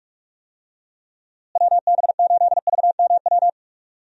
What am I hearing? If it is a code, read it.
Morse code, heard as a keyed tone, about 46 wpm, 700 Hz. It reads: WB8VMW
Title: USA Call Signs - 6 Characters - Repeated 2x - 45wpm